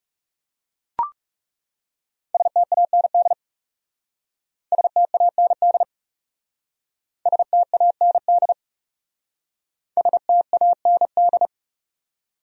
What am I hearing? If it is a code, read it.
Morse code, heard as a keyed tone, about 37 wpm, 700 Hz. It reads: STAND STAND STAND STAND
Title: Speed-Racing Top 300 Words 30wpm